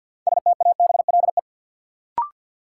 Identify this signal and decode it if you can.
Morse code, heard as a keyed tone, about 46 wpm, 700 Hz. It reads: STABLE
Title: Sentences from Top 1500 Words - New Word Drill - 50wpm